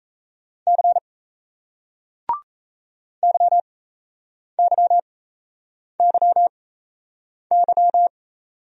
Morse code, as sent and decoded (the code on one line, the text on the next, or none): -.-. / -.-- -.-- -.-- -.--
C YYYY